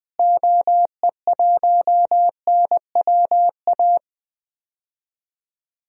Morse code, heard as a keyed tone, about 20 wpm, 700 Hz. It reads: OE1NWA
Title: DXCC-100 Call Signs - 6 Characters - Repeated 2x - 20wpm